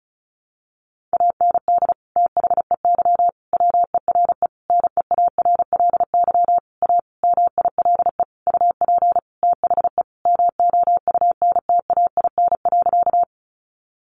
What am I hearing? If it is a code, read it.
Morse code, heard as a keyed tone, about 34 wpm, 700 Hz. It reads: AND THEY WERE NEARLY A MILE UP THE MOUNTAIN.
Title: Sentences from Top 1300 Words 35wpm